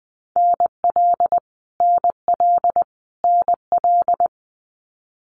Morse code, heard as a keyed tone, about 19 wpm, 700 Hz. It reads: NL NL NL
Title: Canadian Provinces and Territories - Repeated 3x - 20wpm